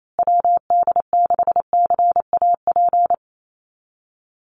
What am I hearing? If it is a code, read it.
Morse code, heard as a keyed tone, about 28 wpm, 700 Hz. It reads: WD6CAP
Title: USA Call Signs Repeated 2x 28wpm